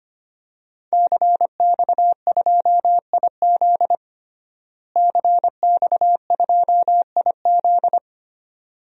Morse code, heard as a keyed tone, about 25 wpm, 700 Hz. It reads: CX2IZ CX2IZ
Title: DXCC-100 Call Signs - 5 Characters - Repeated 2x - 25wpm